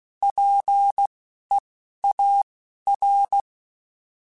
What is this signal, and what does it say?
Morse code, heard as a keyed tone, about 16 wpm, spaced as for 11 wpm, 800 Hz. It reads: PEAR
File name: Alphabetical / Alphabetical mp3s / Ps @10